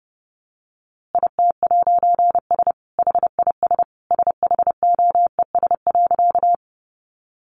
Morse code, heard as a keyed tone, about 28 wpm, 700 Hz. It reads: IT'S HIS SHOES.
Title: A Sound Of Thunder 30wpm